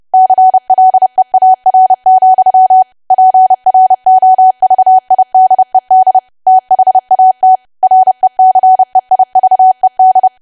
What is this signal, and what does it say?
Morse code, heard as a keyed tone, about 30 wpm, 750 Hz. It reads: CLEAR, PROVIDED THAT RECEIVED
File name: cwa / W1AW-Level3 / 30WPM